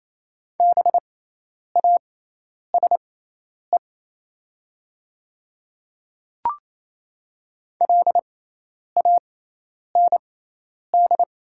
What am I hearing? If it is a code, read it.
Morse code, heard as a keyed tone, about 27 wpm, spaced as for 10 wpm, 700 Hz. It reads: BASE LAND
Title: 6x Character Spacing - Top 300 Words - 28wpm